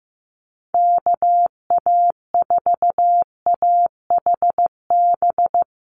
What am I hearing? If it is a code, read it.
Morse code, heard as a keyed tone, about 15 wpm, 700 Hz. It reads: KA4AHB
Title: USA Call Signs Repeated 2x 15wpm